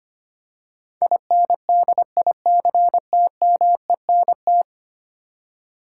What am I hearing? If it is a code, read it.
Morse code, heard as a keyed tone, about 25 wpm, 700 Hz. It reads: INDICTMENT
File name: Ten Letter Words - 25wpm